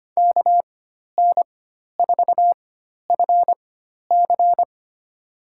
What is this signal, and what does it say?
Morse code, heard as a keyed tone, about 25 wpm, spaced as for 12 wpm, 700 Hz. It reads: KN4FC
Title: USA Call Signs - 4x Character Spacing - 25wpm